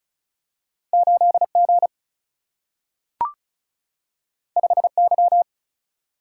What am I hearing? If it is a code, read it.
Morse code, heard as a keyed tone, about 36 wpm, 700 Hz. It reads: 8 G 5 Y
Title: Single-Letter Single-Number 35wpm